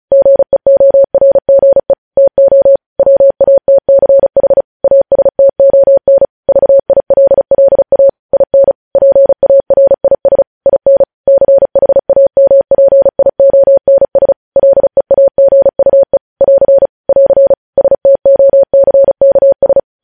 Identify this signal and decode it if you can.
Morse code, heard as a keyed tone, about 36 wpm, 550 Hz. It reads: GEORGE TO WATCH ASTON VILLA IN PARIS IN CHAMPIONS LEAGUE <AR> <AR> STOCKS
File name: podcast / News headlines-1x-35x35